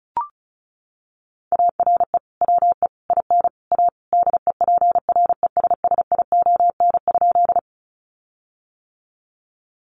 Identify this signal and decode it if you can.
Morse code, heard as a keyed tone, about 34 wpm, 700 Hz. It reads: ARE WE IN A DEPRESSION?
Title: Sentences from Top 1300 Words 35wpm